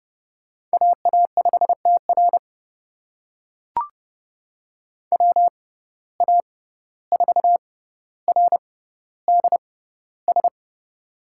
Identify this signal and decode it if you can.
Morse code, heard as a keyed tone, about 31 wpm, spaced as for 15 wpm, 700 Hz. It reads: AA5TR WA4RDS
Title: USA Call Signs - 6x Character Spacing - 30wpm